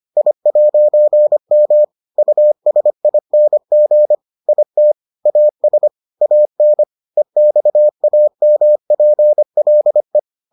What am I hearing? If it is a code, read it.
Morse code, heard as a keyed tone, about 25 wpm, 600 Hz. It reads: I'M USING IT AS AN EXAMPLE